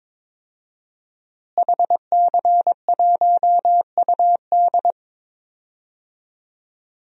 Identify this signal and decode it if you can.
Morse code, heard as a keyed tone, about 22 wpm, 700 Hz. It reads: HC1UD
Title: DXCC-100 Call Signs - 5 Characters - Repeated 3x - 22wpm